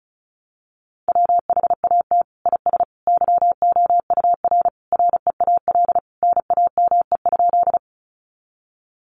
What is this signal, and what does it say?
Morse code, heard as a keyed tone, about 34 wpm, 700 Hz. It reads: WHAT IS YOUR REAL NAME?
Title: Sentences from Top 300 Words 35wpm